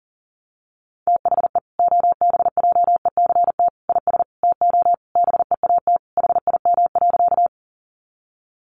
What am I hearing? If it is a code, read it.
Morse code, heard as a keyed tone, about 40 wpm, 700 Hz. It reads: THE OBJECT IS TO BEAT HIM.
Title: Sentences from Top 1300 Words 40wpm